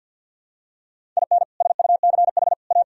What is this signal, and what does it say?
Morse code, heard as a keyed tone, about 50 wpm, 700 Hz. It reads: IN SUCH A
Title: Sentences from Top 1000 Words - Review Part 1 - 50wpm